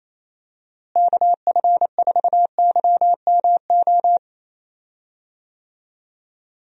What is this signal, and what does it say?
Morse code, heard as a keyed tone, about 28 wpm, 700 Hz. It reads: KF4YMO
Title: USA Call Signs - 6 Characters - Repeated 3x - 28wpm